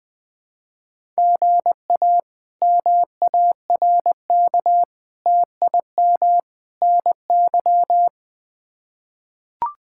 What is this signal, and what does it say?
Morse code, heard as a keyed tone, about 20 wpm, 700 Hz. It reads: GA MARK TIM NY
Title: K1USN SST QSO Practice - 20wpm